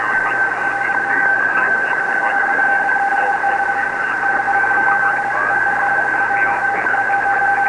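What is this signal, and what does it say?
Two-tone teleprinter signal, about 850 Hz shift.
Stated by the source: Shortwave reception by amateur WA4CJF on a Kenwood TS-520, recorded on a Fostex X-15 four-track cassette recorder.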